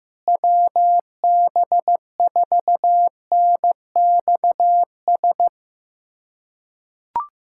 Morse code, heard as a keyed tone, about 15 wpm, 700 Hz. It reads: WB4NXS
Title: K1USN SST QSO Practice - 15wpm